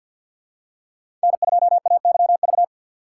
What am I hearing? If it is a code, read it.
Morse code, heard as a keyed tone, about 53 wpm, 700 Hz. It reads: N2AYV
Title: USA Call Signs - 4x Character Spacing - 50wpm